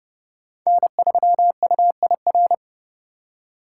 Morse code, heard as a keyed tone, about 28 wpm, 700 Hz. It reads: N3UIR